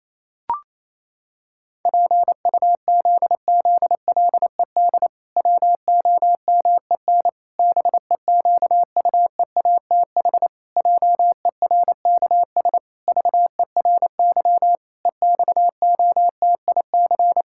Morse code, heard as a keyed tone, about 28 wpm, 700 Hz. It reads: PUZZLED WOMEN BEQUEATH JERKS VERY EXOTIC
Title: Pangram Challenge v2 - 28wpm